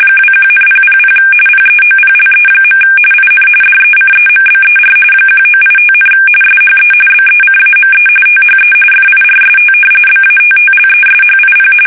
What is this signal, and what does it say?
RTTY, about 45.45 baud. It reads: 21,000 PEOPLE. IN DECEMBER 1985, SHE BECAME THE FIRST BATTLESHIP
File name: fsk850 / battleshipnj